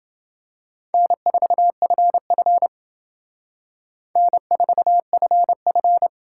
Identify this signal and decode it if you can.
Morse code, heard as a keyed tone, about 29 wpm, 700 Hz. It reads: N4FF N4FF